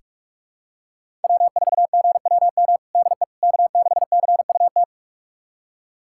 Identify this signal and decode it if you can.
Morse code, heard as a keyed tone, about 46 wpm, 700 Hz. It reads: W4GWM DE K6CUT
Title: USA Callsign DE Callsign - 5 Characters - 45wpm